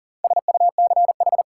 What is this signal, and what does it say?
Morse code, heard as a keyed tone, about 37 wpm, 700 Hz. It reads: SUCH
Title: Sentences from Top 200 Words - New Word Drill - 40wpm